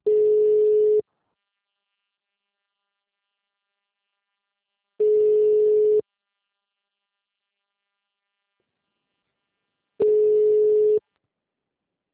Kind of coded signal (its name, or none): telephone call progress tone